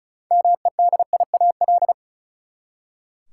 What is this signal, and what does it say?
Morse code, heard as a keyed tone, about 34 wpm, 700 Hz. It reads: MEDIAL
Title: Practice QSO - Exchange 3 - More Info - 35wpm